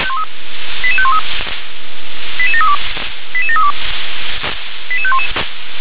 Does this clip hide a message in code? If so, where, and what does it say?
Yes: five-tone selective call (ZVEI) from 0.8 to 5.3 s, heard as quick runs of five tones, tones of about 70 ms. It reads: 88513 88732 88732 88511